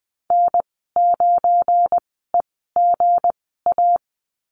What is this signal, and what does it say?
Morse code, heard as a keyed tone, about 20 wpm, spaced as for 15 wpm, 700 Hz. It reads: N9EGA
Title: 219 Lesson 35g - Number 0 - Call Signs 20wpm